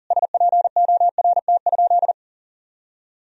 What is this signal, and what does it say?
Morse code, heard as a keyed tone, about 37 wpm, 700 Hz. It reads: SPORT?
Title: Sentences from Top 1000 Words - Review Part 4 - 40wpm